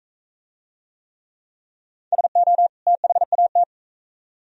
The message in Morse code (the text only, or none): SO THAT